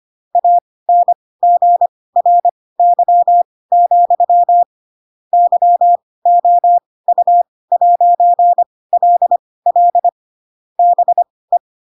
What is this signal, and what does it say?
Morse code, heard as a keyed tone, about 25 wpm, spaced as for 18 wpm, 700 Hz. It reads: ANGRY, YOU'LL BE
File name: Pinocchio / Pinocchio ch4 12 25 ik1hge